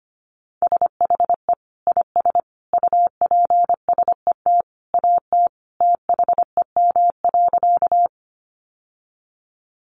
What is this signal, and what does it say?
Morse code, heard as a keyed tone, about 25 wpm, 700 Hz. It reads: SHE IS UPSET AT THEM.
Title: Sentences from Top 800 Words 25wpm